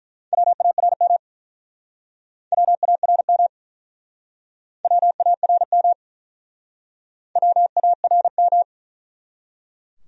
Morse code, heard as a keyed tone, about 49 wpm, 700 Hz. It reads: WARM WARM WARM WARM